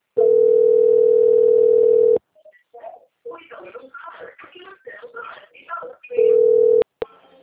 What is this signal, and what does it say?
Ringback tone (North America), 440 + 480 Hz, with speech.
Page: Cosmopolis - SP